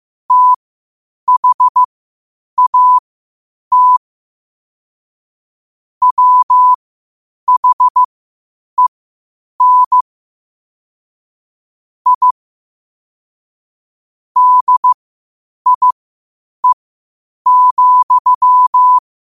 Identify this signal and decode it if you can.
Morse code, heard as a keyed tone, about 15 wpm, spaced as for 8 wpm, 1000 Hz. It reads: THAT WHEN I DIE,
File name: qotdfiles / 8wpm